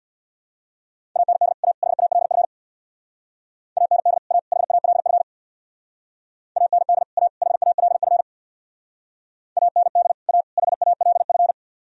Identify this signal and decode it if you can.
Morse code, heard as a keyed tone, about 60 wpm, 700 Hz. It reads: AND A HALF AND A HALF AND A HALF AND A HALF
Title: Speed-Racing - Sets of 3 Words 50wpm